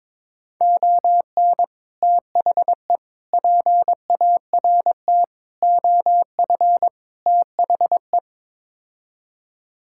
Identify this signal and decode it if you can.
Morse code, heard as a keyed tone, about 22 wpm, 700 Hz. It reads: ON THE PART OF THE
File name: practice / 22wpm / Build A Phrase - 5-Word Phrases - 22wpm